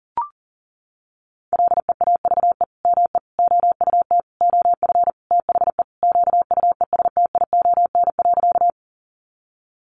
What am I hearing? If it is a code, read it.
Morse code, heard as a keyed tone, about 37 wpm, 700 Hz. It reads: LEAVE ME OUT OF THE QUESTION.
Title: Sentences from Top 400 Words 40wpm